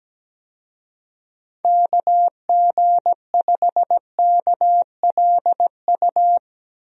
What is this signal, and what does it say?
Morse code, heard as a keyed tone, about 17 wpm, 700 Hz. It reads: KG5KLU
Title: USA Call Signs - 6 Characters - Repeated 2x - 17wpm